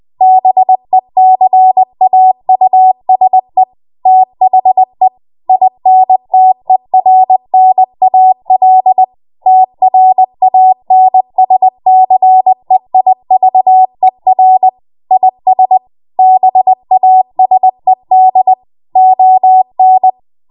Morse code, heard as a keyed tone, about 20 wpm, 750 Hz. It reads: BECAUSE THE INTERNAL TRANSCEIVER IS BASED ON
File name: Morse / 20WPM